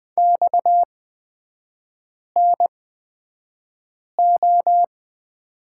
Morse code, heard as a keyed tone, about 20 wpm, spaced as for 5 wpm, 690 Hz. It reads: XNO
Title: 229 Lesson 37e - Letter x - Three Characters 20wpm